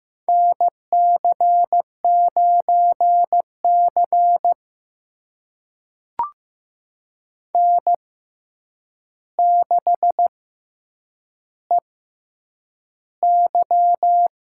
Morse code, heard as a keyed tone, about 15 wpm, spaced as for 8 wpm, 700 Hz. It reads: NC9C N6EY
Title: USA Call Signs - 6x Character Spacing - 15wpm